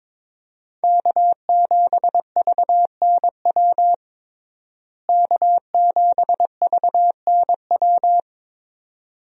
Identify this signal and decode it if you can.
Morse code, heard as a keyed tone, about 22 wpm, 700 Hz. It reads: K7VNW K7VNW
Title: USA Call Signs - 5 Characters - Repeated 2x - 22wpm